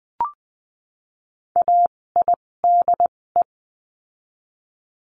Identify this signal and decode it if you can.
Morse code, heard as a keyed tone, about 20 wpm, spaced as for 16 wpm, 700 Hz. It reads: AIDE
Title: 256 Lesson 42a - Words 12-20wpm